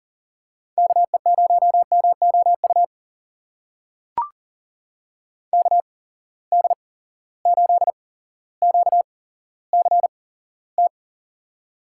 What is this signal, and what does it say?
Morse code, heard as a keyed tone, about 40 wpm, spaced as for 17 wpm, 700 Hz. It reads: KE0MOU KD8QCT
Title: USA Call Signs - 6 Characters - 8x Character Spacing - 40wpm